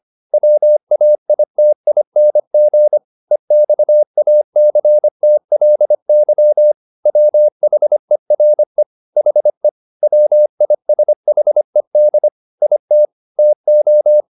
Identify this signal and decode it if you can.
Morse code, heard as a keyed tone, about 25 wpm, 600 Hz. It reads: WAITING EXACTLY WHERE HE WISHED IT TO
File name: the old man and the sea part 1 25wpm